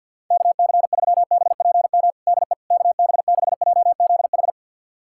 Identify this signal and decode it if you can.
Morse code, heard as a keyed tone, about 50 wpm, 700 Hz. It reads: KC3BPM DE KB6JZH
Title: USA Callsign DE Callsign - 6 Characters - 50wpm